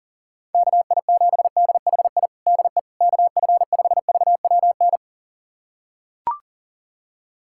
Morse code, heard as a keyed tone, about 40 wpm, 700 Hz. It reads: KI7DHI DE KF5VWN
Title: USA Callsign DE Callsign - 6 Characters - 40wpm